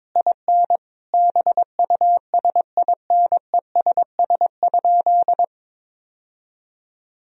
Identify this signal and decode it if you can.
Morse code, heard as a keyed tone, about 22 wpm, 700 Hz. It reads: IN BUSINESS?
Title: Sentences from Top 1000 Words - Review Part 3 - 22wpm